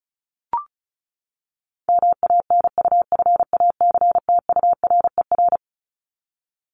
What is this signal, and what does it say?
Morse code, heard as a keyed tone, about 34 wpm, 700 Hz. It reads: MANUFACTURER
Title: Top 1751-2000 Words 35wpm